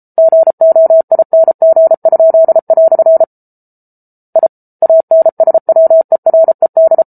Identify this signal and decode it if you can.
Morse code, heard as a keyed tone, about 34 wpm, 650 Hz. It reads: GOING?" I ANSWERED